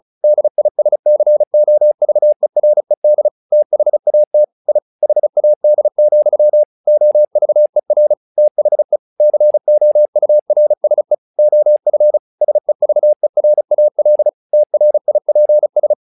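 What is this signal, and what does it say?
Morse code, heard as a keyed tone, about 35 wpm, 600 Hz. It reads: DISCOVERED THAT I HAD, OVER THE COURSE OF SEVERAL TRIPS
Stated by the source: Morse practice text "Tower Trouble" sent at 35 WPM, from the SKCC - Straight Key Century Club learning center audio files.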